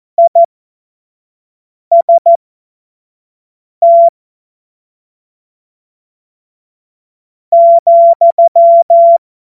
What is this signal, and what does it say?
Morse code, heard as a keyed tone, about 14 wpm, spaced as for 4 wpm, 680 Hz. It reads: IST,
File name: Morsekurs / Witze / Witz 21. Februar 2023 25 F70